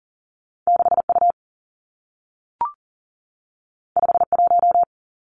Morse code, heard as a keyed tone, about 39 wpm, 700 Hz. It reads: 6U 51
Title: Two Challenging Letter-Number Combinations 40wpm